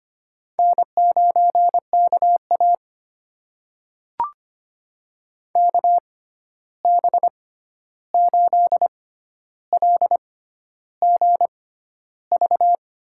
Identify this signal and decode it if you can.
Morse code, heard as a keyed tone, about 25 wpm, spaced as for 11 wpm, 700 Hz. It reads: N9KA KB8LGV
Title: USA Call Signs - 6x Character Spacing - 25wpm